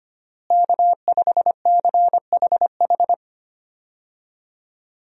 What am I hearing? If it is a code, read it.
Morse code, heard as a keyed tone, about 25 wpm, 700 Hz. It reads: K5CHH
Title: USA Call Signs - 5 Characters - Repeated 3x - 25wpm